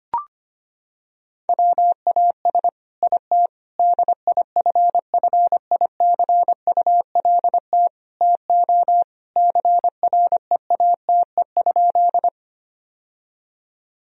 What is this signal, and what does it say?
Morse code, heard as a keyed tone, about 25 wpm, 700 Hz. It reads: WAS IT DIFFICULT TO CREATE?
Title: Sentences from Top 1000 Words - Review Part 3 - 25wpm